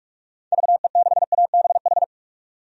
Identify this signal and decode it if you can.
Morse code, heard as a keyed tone, about 46 wpm, 700 Hz. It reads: VE6ABH
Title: DXCC-100 Call Signs - 6 Characters - 45wpm